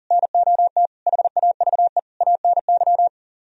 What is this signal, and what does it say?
Morse code, heard as a keyed tone, about 47 wpm, 700 Hz. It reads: NOT HAVE ANY
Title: Speed-Racing - Sets of 5 Words 40wpm